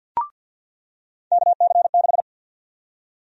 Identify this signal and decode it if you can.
Morse code, heard as a keyed tone, about 46 wpm, 700 Hz. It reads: KC6
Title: USA Call Sign Prefixes - 50wpm